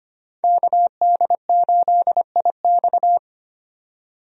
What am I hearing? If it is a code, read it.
Morse code, heard as a keyed tone, about 25 wpm, 700 Hz. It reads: KD8IX
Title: USA Call Signs - 5 Characters - 25wpm